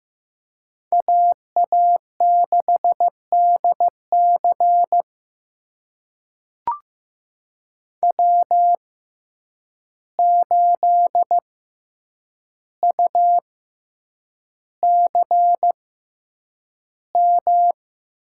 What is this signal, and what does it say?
Morse code, heard as a keyed tone, about 15 wpm, spaced as for 8 wpm, 700 Hz. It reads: AA6DC W8UCM
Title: USA Call Signs - 5 Characters - 6x Character Spacing - 15wpm